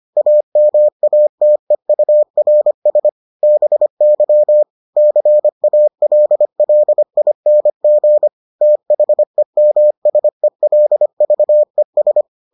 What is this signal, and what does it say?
Morse code, heard as a keyed tone, about 25 wpm, 600 Hz. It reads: AMATEURS BY CALLING THEMSELVES